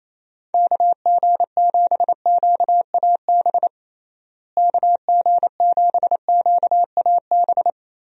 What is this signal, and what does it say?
Morse code, heard as a keyed tone, about 28 wpm, 700 Hz. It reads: KG7QAB KG7QAB
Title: USA Call Signs - 6 Characters - Repeated 3x - 28wpm